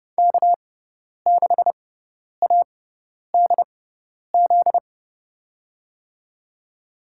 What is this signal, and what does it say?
Morse code, heard as a keyed tone, about 31 wpm, spaced as for 10 wpm, 690 Hz. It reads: K6ADZ
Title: USA Call Signs - 5 Characters - 6x Character Spacing - 30wpm